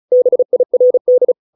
Morse code, heard as a keyed tone, about 34 wpm, 500 Hz. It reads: BIRD